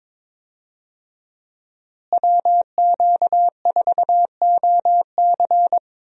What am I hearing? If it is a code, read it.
Morse code, heard as a keyed tone, about 22 wpm, 700 Hz. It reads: WQ4OC
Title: USA Call Signs - 5 Characters - Repeated 3x - 22wpm